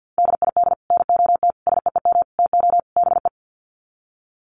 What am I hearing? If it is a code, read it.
Morse code, heard as a keyed tone, about 50 wpm, 700 Hz. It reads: DID NOT SEEM TO BE
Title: Sets of 5 Words Encore 50wpm